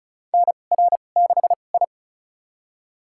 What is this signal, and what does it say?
Morse code, heard as a keyed tone, about 34 wpm, spaced as for 25 wpm, 700 Hz. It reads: NR6I
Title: USA Call Signs - 4 Characters - 2x Character Spacing - 35wpm